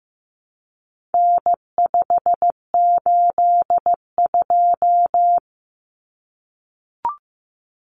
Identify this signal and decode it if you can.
Morse code, heard as a keyed tone, about 15 wpm, 700 Hz. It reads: N582
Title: Common Cut Numbers - 4-Digit 15wpm